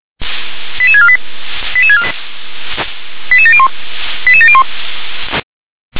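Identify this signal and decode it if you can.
Five-tone selective call (ZVEI), heard as quick runs of five tones, tones of about 70 ms. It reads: 88648 88701 88701
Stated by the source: ZVEI five-tone call 88725 received by SDR.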